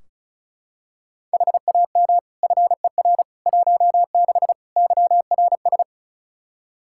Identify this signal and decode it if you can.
Morse code, heard as a keyed tone, about 36 wpm, 700 Hz. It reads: HAM FER 16 YRS
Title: Practice QSO - Exchange 3 - More Info - 35wpm